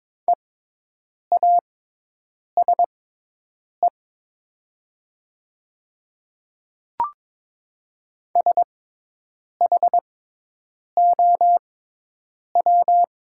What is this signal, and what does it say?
Morse code, heard as a keyed tone, about 22 wpm, spaced as for 8 wpm, 700 Hz. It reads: EASE SHOW
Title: 6x Character Spacing - Top 500 Words - 22wpm